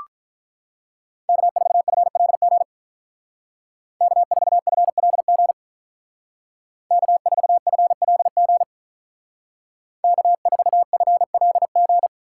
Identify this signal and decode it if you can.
Morse code, heard as a keyed tone, about 43 wpm, 700 Hz. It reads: K4FLG K4FLG K4FLG K4FLG
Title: Speed Racing - USA 1x3 Call Signs 35wpm